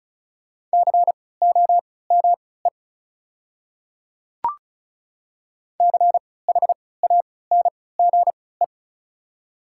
Morse code, heard as a keyed tone, about 36 wpm, spaced as for 20 wpm, 700 Hz. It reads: COME CHANGE
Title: 3x Character Spacing - Top 300 Words - 35wpm